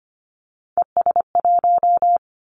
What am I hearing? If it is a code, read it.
Morse code, heard as a keyed tone, about 25 wpm, 700 Hz. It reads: ES1
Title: Callsign Prefixes for Top 100 DXCC - 25wpm